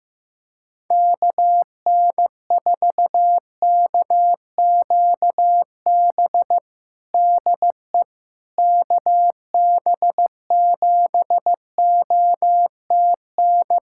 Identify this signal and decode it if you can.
Morse code, heard as a keyed tone, about 15 wpm, 700 Hz. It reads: KN4KQB DE KB7OTN
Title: USA Callsign DE Callsign 15wpm